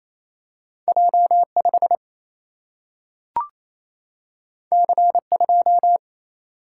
Morse code, heard as a keyed tone, about 28 wpm, 700 Hz. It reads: J 5 C 2
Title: Single-Letter Single-Number 28wpm